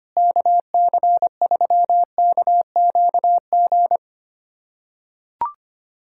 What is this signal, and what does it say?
Morse code, heard as a keyed tone, about 25 wpm, 700 Hz. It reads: KC3KQG
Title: USA Callsign DE Callsign - 6 Characters - 25wpm